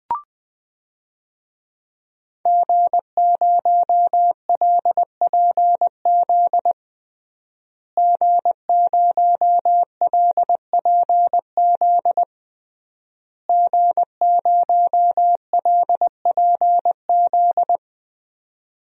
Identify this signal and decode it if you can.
Morse code, heard as a keyed tone, about 21 wpm, 700 Hz. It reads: G0LPZ G0LPZ G0LPZ
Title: DXCC-100 Call Signs - 5 Characters - Repeated 3x - 20wpm